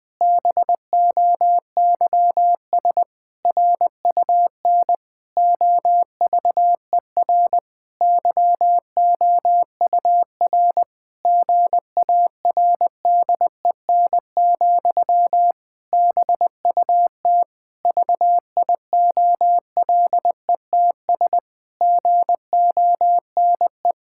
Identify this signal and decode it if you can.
Morse code, heard as a keyed tone, about 20 wpm, 700 Hz. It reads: BOYS RUN OVER YOUR GARDEN, BUT VIOLETS GONE
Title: Pangram Challenge v2 - 20wpm